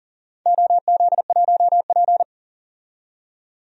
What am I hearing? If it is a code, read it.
Morse code, heard as a keyed tone, about 40 wpm, 700 Hz. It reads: OZ1P
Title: DXCC-100 Call Signs - 4 Characters - 40wpm